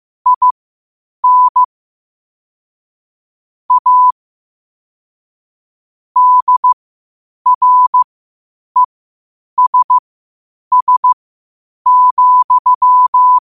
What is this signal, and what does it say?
Morse code, heard as a keyed tone, about 15 wpm, spaced as for 8 wpm, 1000 Hz. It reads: IN A DRESS,
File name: qotdfiles / 8wpm